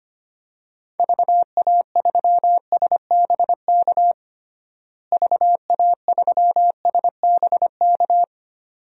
Morse code, heard as a keyed tone, about 25 wpm, 700 Hz. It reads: VA3SBK VA3SBK
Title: DXCC-100 Call Signs - 6 Characters - Repeated 2x - 25wpm